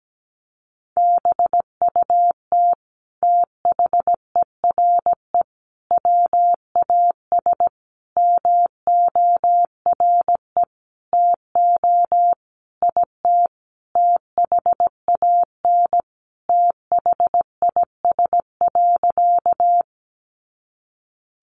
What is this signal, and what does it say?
Morse code, heard as a keyed tone, about 17 wpm, 700 Hz. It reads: BUT THERE WAS MORE TO IT THAN THIS.
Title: Sentences from Top 100 Words 17wpm